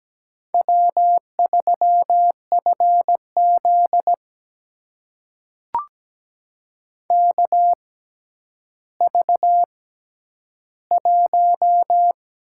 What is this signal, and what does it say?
Morse code, heard as a keyed tone, about 17 wpm, spaced as for 10 wpm, 700 Hz. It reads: W3FZ KV1
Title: USA Call Signs - 4 Characters - 6x Character Spacing - 17wpm